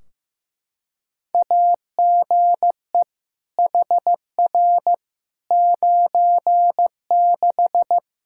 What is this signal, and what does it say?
Morse code, heard as a keyed tone, about 15 wpm, 700 Hz. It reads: AGE HR 96
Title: Practice QSO - Exchange 3 - More Info - 15wpm